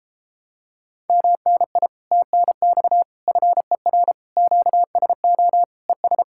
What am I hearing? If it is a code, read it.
Morse code, heard as a keyed tone, about 32 wpm, 700 Hz. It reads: MNI TNX FER QSO ES